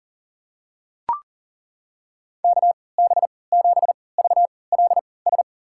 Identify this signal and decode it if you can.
Morse code, heard as a keyed tone, about 40 wpm, spaced as for 23 wpm, 700 Hz. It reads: KB7VLS
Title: USA Call Signs - 6 Characters - 3x Character Spacing - 40wpm